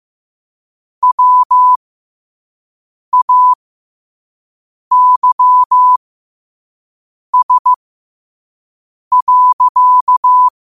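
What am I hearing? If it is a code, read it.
Morse code, heard as a keyed tone, about 15 wpm, spaced as for 5 wpm, 1000 Hz. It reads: WAYS.
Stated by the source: Morse code sent at 5 wpm for learning.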